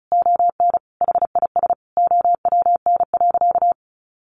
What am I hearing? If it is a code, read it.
Morse code, heard as a keyed tone, about 34 wpm, 700 Hz. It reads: ON HIS OWN.
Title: Sentences from Top 1400 Words 35wpm